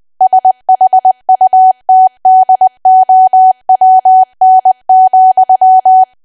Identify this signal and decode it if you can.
Morse code, heard as a keyed tone, about 20 wpm, 750 Hz. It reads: SHUTDOWN,
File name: cwa / W1AW-Level3 / 20WPM